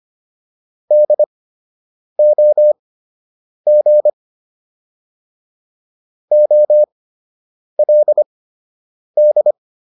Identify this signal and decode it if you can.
Morse code, heard as a keyed tone, about 25 wpm, spaced as for 8 wpm, 600 Hz. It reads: DOG OLD